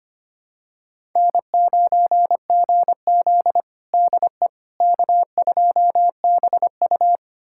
Morse code, heard as a keyed tone, about 25 wpm, 700 Hz. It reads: N9GZ DE K2BU
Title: USA Callsign DE Callsign - 4 Characters - 25wpm